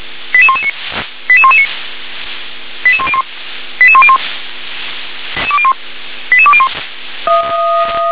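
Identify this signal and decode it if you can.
Five-tone selective call (ZVEI), heard as quick runs of five tones, tones of about 70 ms. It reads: 88119 88119 88101 88101 88201
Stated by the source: ZVEI five-tone selective call received on an SDR.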